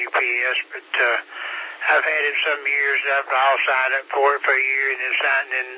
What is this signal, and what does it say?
Emergency Alert System (SAME) tones (header bursts), heard as harsh buzzing bursts, with speech. It is heard from 0.7 to 4.9 s.